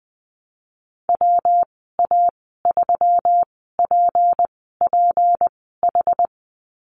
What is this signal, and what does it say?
Morse code, heard as a keyed tone, about 20 wpm, spaced as for 15 wpm, 700 Hz. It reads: WA3PPH
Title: 213 Lesson 34g - Number 8 - Call Signs 20wpm